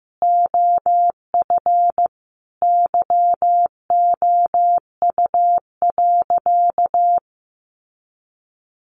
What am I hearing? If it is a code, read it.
Morse code, heard as a keyed tone, about 15 wpm, 700 Hz. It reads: OF YOU.
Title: Sentences from Top 200 Words 15wpm